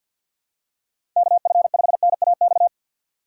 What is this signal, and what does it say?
Morse code, heard as a keyed tone, about 46 wpm, 700 Hz. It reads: KF5NAX
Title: USA Call Signs - 4x Character Spacing - 50wpm